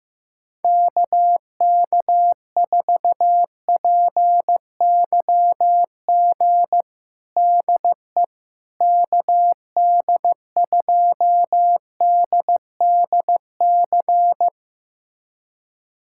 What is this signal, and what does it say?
Morse code, heard as a keyed tone, about 16 wpm, 700 Hz. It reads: KK4PYG DE KD2DDC
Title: USA Callsign DE Callsign 15wpm